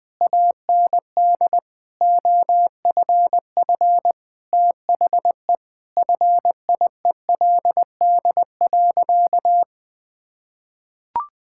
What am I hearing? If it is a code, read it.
Morse code, heard as a keyed tone, about 21 wpm, 700 Hz. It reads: AND OFF THE FIELD.
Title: Sentences with 6-Word Phrases - 20wpm